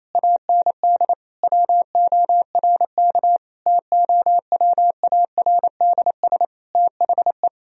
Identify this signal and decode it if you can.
Morse code, heard as a keyed tone, about 28 wpm, 700 Hz. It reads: AND WORK TOWARDS THE
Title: Sentences from Top 1000 Words - Review Part 3 - 28wpm